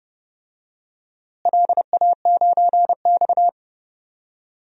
Morse code, heard as a keyed tone, about 30 wpm, 700 Hz. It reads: LA9X
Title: DXCC-100 Call Signs - 4 Characters - Repeated 2x - 30wpm